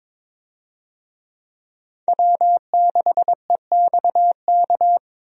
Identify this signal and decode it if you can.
Morse code, heard as a keyed tone, about 22 wpm, 700 Hz. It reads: W6EXK
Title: USA Call Signs - 5 Characters - Repeated 2x - 22wpm